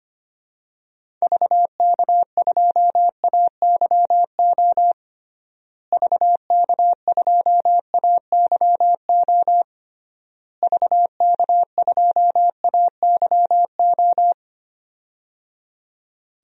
Morse code, heard as a keyed tone, about 25 wpm, 700 Hz. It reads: VK2AYO VK2AYO VK2AYO
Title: DXCC-100 Call Signs - 6 Characters - Repeated 3x - 25wpm